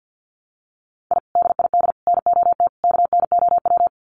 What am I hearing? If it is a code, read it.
Morse code, heard as a keyed tone, about 50 wpm, 700 Hz. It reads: I DID NOT KNOW
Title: Sets of 4 Words 50wpm